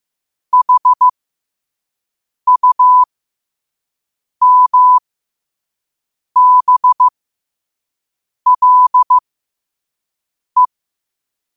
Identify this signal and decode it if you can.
Morse code, heard as a keyed tone, about 15 wpm, spaced as for 5 wpm, 1000 Hz. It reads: HUMBLE
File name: qotdfiles / 5wpm